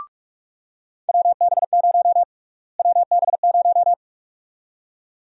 Morse code, heard as a keyed tone, about 42 wpm, 700 Hz. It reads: WB0 WB0
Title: USA Call Sign Prefixes - Repeated 2x - 45wpm